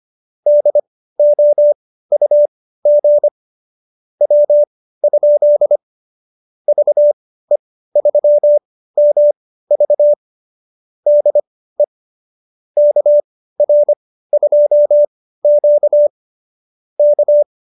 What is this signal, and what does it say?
Morse code, heard as a keyed tone, about 25 wpm, spaced as for 15 wpm, 590 Hz. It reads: DOUG W? VE3MV DE KR2Q K